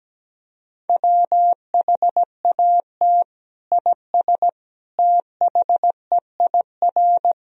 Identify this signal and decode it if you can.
Morse code, heard as a keyed tone, about 17 wpm, 700 Hz. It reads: WHAT IS THEIR
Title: Sentences from Top 1000 Words - Review Part 2 - 17wpm